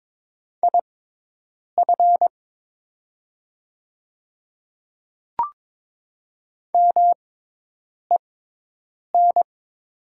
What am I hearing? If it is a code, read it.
Morse code, heard as a keyed tone, about 22 wpm, spaced as for 8 wpm, 700 Hz. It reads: IF MEN
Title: 6x Character Spacing - Top 200 Words - 22wpm